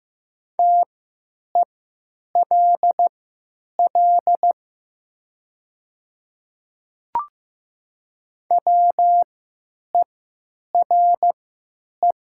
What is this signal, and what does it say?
Morse code, heard as a keyed tone, about 15 wpm, spaced as for 9 wpm, 700 Hz. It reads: TELL WERE
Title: 3x Character Spacing - Top 300 Words - 15wpm